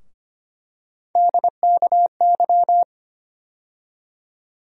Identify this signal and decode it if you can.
Morse code, heard as a keyed tone, about 25 wpm, 700 Hz. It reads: DKY
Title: USA Call Sign Postfixes - 3 Letters 25wpm